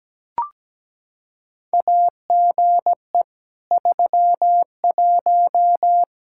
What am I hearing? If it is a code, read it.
Morse code, heard as a keyed tone, about 17 wpm, 700 Hz. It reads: AGE 31
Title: Practice QSO - Exchange 3 - More Info - 17wpm